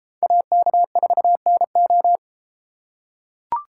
Morse code, heard as a keyed tone, about 33 wpm, 700 Hz. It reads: AK4NO